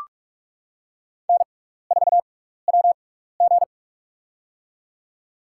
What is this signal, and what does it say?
Morse code, heard as a keyed tone, about 44 wpm, spaced as for 15 wpm, 700 Hz. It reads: N4WG
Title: USA Call Signs - 4 Characters - 6x Character Spacing - 45wpm